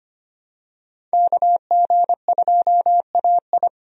Morse code, heard as a keyed tone, about 25 wpm, 700 Hz. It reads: KG2AI
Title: USA Call Signs - 5 Characters - 25wpm